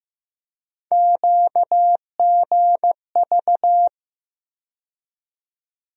Morse code, heard as a keyed tone, about 15 wpm, 700 Hz. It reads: QGV
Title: USA Call Sign Postfixes - 3 Letters 15wpm